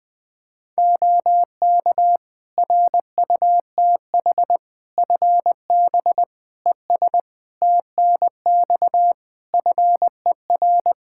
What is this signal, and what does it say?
Morse code, heard as a keyed tone, about 21 wpm, 700 Hz. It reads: OK RUTH FB ES TNX FER
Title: Practice QSO - Exchange 2 - Info - 20wpm